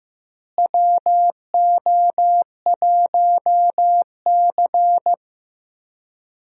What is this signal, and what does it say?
Morse code, heard as a keyed tone, about 15 wpm, 700 Hz. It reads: WO1C
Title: USA Call Signs - 4 Characters - Repeated 3x - 15wpm